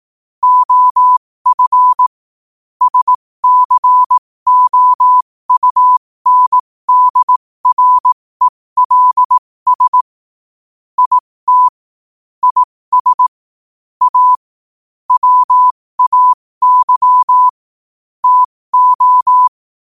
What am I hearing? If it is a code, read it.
Morse code, heard as a keyed tone, about 18 wpm, spaced as for 15 wpm, 1000 Hz. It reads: OF SCOUNDRELS IT IS A WAY TO
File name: qotdfiles / 15wpm